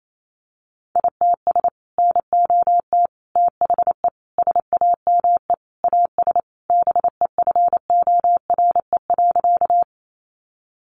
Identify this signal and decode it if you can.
Morse code, heard as a keyed tone, about 28 wpm, 700 Hz. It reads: ITS NOT THE SAME AS BEFORE.
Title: Sentences from Top 600 Words 28wpm